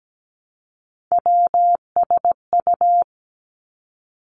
Morse code, heard as a keyed tone, about 17 wpm, 700 Hz. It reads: WSU
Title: Three Challenging Letter-Number Combinations 17wpm